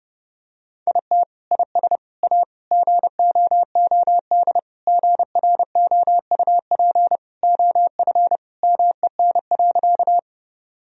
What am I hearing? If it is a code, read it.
Morse code, heard as a keyed tone, about 30 wpm, 700 Hz. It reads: IT IS A GOOD GROUP OF MEN.
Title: Sentences from Top 1000 Words - Review Part 1 - 30wpm